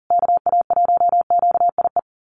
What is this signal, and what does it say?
Morse code, heard as a keyed tone, about 40 wpm, 700 Hz. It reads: KA1QIE
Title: USA Call Signs Repeated 3x 40wpm